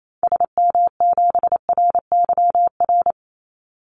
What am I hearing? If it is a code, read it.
Morse code, heard as a keyed tone, about 28 wpm, 700 Hz. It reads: SM7RYR
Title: Call Signs Repeated 2x - 28wpm